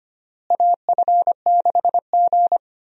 Morse code, heard as a keyed tone, about 25 wpm, 700 Hz. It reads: AF6G
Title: USA Call Signs - 4 Characters - Repeated 2x - 25wpm